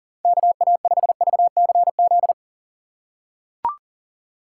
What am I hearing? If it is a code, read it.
Morse code, heard as a keyed tone, about 37 wpm, 700 Hz. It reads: KA5VCZ
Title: USA Call Signs - 6 Characters - 2x Character Spacing - 40wpm